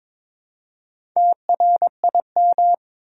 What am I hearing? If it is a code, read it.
Morse code, heard as a keyed tone, about 22 wpm, 700 Hz. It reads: TRIM
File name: practice / 22wpm / Easy Sentences with Hints - 22wpm